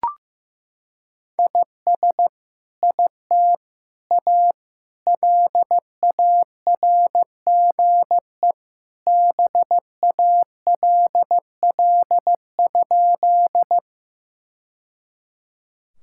Morse code, heard as a keyed tone, about 15 wpm, 700 Hz. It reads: IS IT A LARGE BALL?
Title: Sentences from Top 1000 Words - Review Part 6 - 15wpm